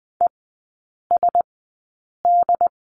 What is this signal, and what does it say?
Morse code, heard as a keyed tone, about 20 wpm, spaced as for 8 wpm, 700 Hz. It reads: ESD
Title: 067 Lesson 12e - Letter d - Three Characters 20wpm